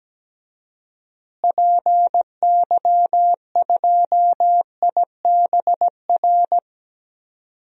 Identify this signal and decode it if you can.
Morse code, heard as a keyed tone, about 17 wpm, 700 Hz. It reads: PY2IBR
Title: DXCC-100 Call Signs - 6 Characters - Repeated 2x - 17wpm